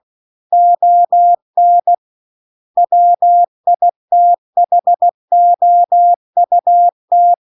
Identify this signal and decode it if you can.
Morse code, heard as a keyed tone, about 16 wpm, 700 Hz. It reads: ON WITHOUT